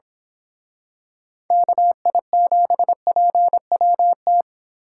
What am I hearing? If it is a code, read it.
Morse code, heard as a keyed tone, about 26 wpm, 700 Hz. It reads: KI7PWT
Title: Speed Racing - USA 2x3 Call Signs 22wpm